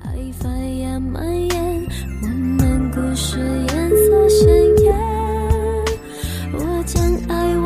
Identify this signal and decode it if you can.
Ringback tone (Europe, 425 Hz), with noise.